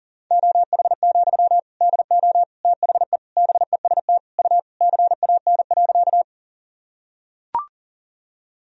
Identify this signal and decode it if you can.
Morse code, heard as a keyed tone, about 40 wpm, 700 Hz. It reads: OH, DO THE BEST U CAN.
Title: Sentences from Top 500 Words and Abbreviations 40wpm